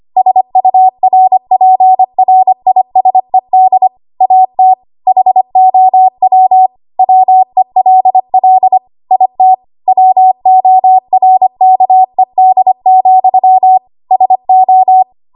Morse code, heard as a keyed tone, about 25 wpm, 750 Hz. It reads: SURPRISED AT HOW WELL IT WORKED, SO
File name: Archive / 25 WPM / 25WPM